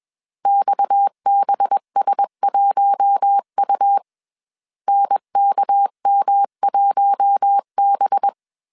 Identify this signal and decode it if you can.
Morse code, heard as a keyed tone, about 21 wpm, 800 Hz. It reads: XBS1U NKM1B
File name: CW / DET / 1 6 1500C S B 20wpm 2048